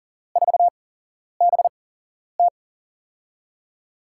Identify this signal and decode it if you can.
Morse code, heard as a keyed tone, about 39 wpm, spaced as for 11 wpm, 700 Hz. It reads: 4BT
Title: USA Call Signs - 5 Characters - 8x Character Spacing - 40wpm